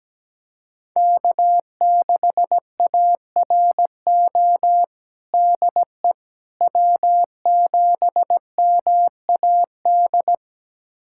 Morse code, heard as a keyed tone, about 17 wpm, 700 Hz. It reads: K6ARO DE W7MAD
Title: USA Callsign DE Callsign - 5 Characters - 17wpm